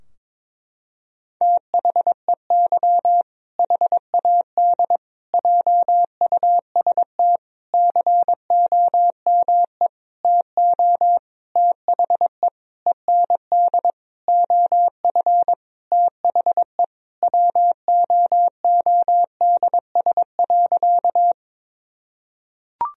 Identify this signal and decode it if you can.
Morse code, heard as a keyed tone, about 22 wpm, 700 Hz. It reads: THEY HAD JUST COME TO THE END OF THE WOODS.
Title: Sentences from Top 1000 Words - Review Part 2 - 22wpm